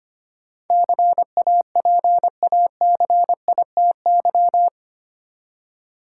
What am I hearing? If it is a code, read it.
Morse code, heard as a keyed tone, about 25 wpm, 690 Hz. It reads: CAPACITY
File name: 8 Letter Words - First 3 Letters Given - 25wpm